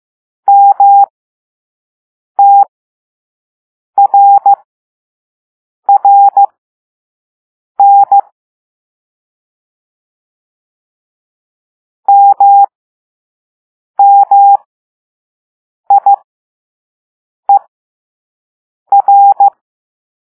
Morse code, heard as a keyed tone, about 15 wpm, spaced as for 5 wpm, 800 Hz. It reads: MTRRN MMIER